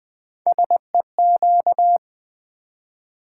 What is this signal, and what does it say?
Morse code, heard as a keyed tone, about 20 wpm, 700 Hz. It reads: SEQ
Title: Build A Word - 3 Letters At A Time - 8-Letter Words - 20wpm